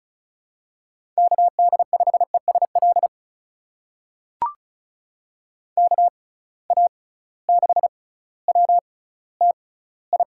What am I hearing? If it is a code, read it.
Morse code, heard as a keyed tone, about 36 wpm, spaced as for 18 wpm, 700 Hz. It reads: KD5ESL KA6WTI
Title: USA Call Signs - 6 Characters - 6x Character Spacing - 35wpm